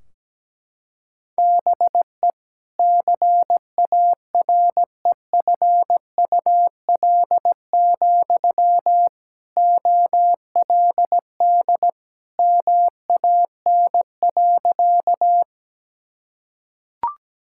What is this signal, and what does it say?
Morse code, heard as a keyed tone, about 17 wpm, 700 Hz. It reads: BE CAREFUL, OLD MAN.
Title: Sentences from Top 1600 Words 17wpm